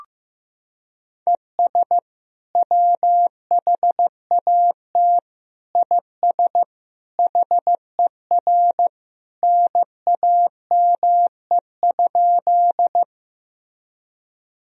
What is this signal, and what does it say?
Morse code, heard as a keyed tone, about 15 wpm, 700 Hz. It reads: ES WHAT IS HER NAME?
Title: Sentences from Top 200 Words and Abbreviations 15wpm